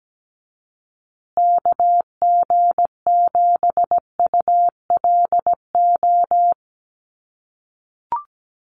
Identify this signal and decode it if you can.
Morse code, heard as a keyed tone, about 17 wpm, 700 Hz. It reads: KG7ULO
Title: USA Call Signs Repeated 3x 17wpm